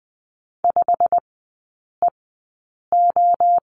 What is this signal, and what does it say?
Morse code, heard as a keyed tone, about 20 wpm, spaced as for 8 wpm, 690 Hz. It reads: 5EO
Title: 087 Lesson 15e - Number 5 - Three Characters 20wpm